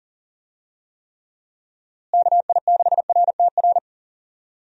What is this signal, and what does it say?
Morse code, heard as a keyed tone, about 40 wpm, 700 Hz. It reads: KI6RTR
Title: USA Call Signs - 6 Characters - Repeated 2x - 40wpm